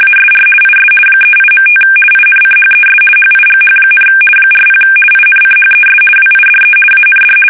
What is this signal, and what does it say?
RTTY, about 50 baud. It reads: ASR IS 38.5 INCHES HIGH, 40 INCHES WIDE AND
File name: fsk850-50B / teletype-corp